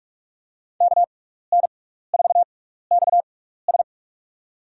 Morse code, heard as a keyed tone, about 48 wpm, spaced as for 16 wpm, 700 Hz. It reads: KN4XS
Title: USA Call Signs - 5 Characters - 6x Character Spacing - 45wpm